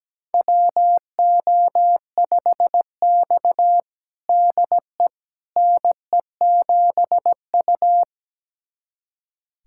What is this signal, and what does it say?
Morse code, heard as a keyed tone, about 17 wpm, 700 Hz. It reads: WO5X DE NE7U
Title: USA Callsign DE Callsign - 4 Characters - 17wpm